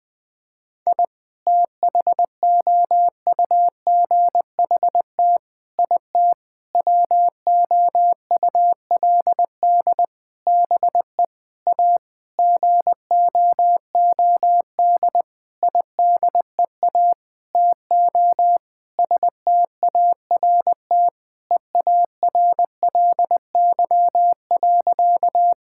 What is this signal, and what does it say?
Morse code, heard as a keyed tone, about 20 wpm, 700 Hz. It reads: I THOUGHT IT WOULD BE A GOOD IDEA TO START EARLY.
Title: Sentences with 9-Word Phrases - 20wpm